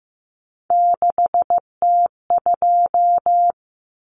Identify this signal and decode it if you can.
Morse code, heard as a keyed tone, about 15 wpm, 700 Hz. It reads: 6T2
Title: Common Cut Numbers - 3-Digit 15wpm